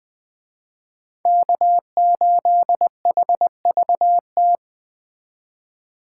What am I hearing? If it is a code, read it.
Morse code, heard as a keyed tone, about 20 wpm, 700 Hz. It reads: K8HVT
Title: Speed Racing - USA 1x3 Call Signs 17wpm